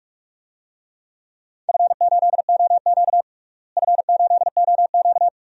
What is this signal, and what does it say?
Morse code, heard as a keyed tone, about 42 wpm, 700 Hz. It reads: F8OQ F8OQ
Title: DXCC-100 Call Signs - 4 Characters - Repeated 3x - 45wpm